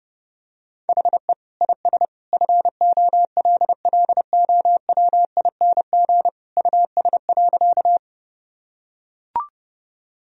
Morse code, heard as a keyed tone, about 31 wpm, 700 Hz. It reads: HE IS FOLLOWING US.
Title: Sentences from Top 1000 Words - Review Part 4 - 30wpm